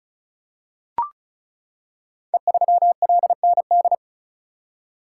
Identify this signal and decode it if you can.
Morse code, heard as a keyed tone, about 36 wpm, 700 Hz. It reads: E3LND
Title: Code Groups - 5 Letter:Numbers - 35wpm